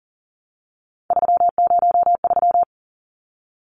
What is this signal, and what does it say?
Morse code, heard as a keyed tone, about 40 wpm, 700 Hz. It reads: 303
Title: Numbers - 3-Digit 40wpm